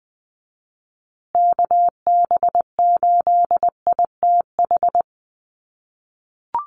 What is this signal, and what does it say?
Morse code, heard as a keyed tone, about 20 wpm, 700 Hz. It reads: KB8ITH
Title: USA Call Signs Repeated 2x 20wpm